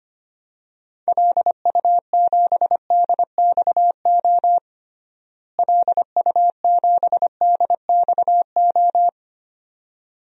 Morse code, heard as a keyed tone, about 25 wpm, 700 Hz. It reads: LU7DXO LU7DXO
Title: DXCC-100 Call Signs - 6 Characters - Repeated 2x - 25wpm